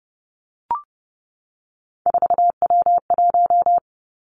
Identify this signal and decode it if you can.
Morse code, heard as a keyed tone, about 30 wpm, 700 Hz. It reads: 4W1
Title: Three Challenging Letter-Number Combinations 30wpm